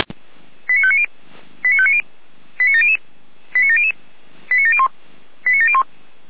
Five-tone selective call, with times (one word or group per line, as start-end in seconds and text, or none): 0.7-1.1 89590
1.6-2.0 89590
2.6-3.0 89700
3.5-3.9 89700
4.5-4.9 89701
5.5-5.8 89701